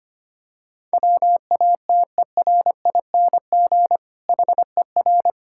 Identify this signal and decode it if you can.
Morse code, heard as a keyed tone, about 25 wpm, 700 Hz. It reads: WATERING HER
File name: practice / 25wpm / Difficult Sentences with Hints - 25wpm